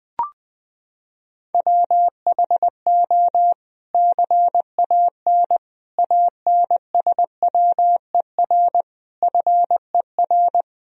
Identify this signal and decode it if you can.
Morse code, heard as a keyed tone, about 20 wpm, 700 Hz. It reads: WHO CAN ANSWER FER
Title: Sentences from Top 300 Words and Abbreviations 20wpm